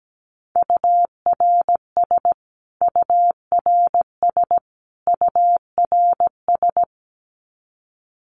Morse code, heard as a keyed tone, about 17 wpm, 700 Hz. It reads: URS URS URS
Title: Abbreviations Repeated 3x - 17wpm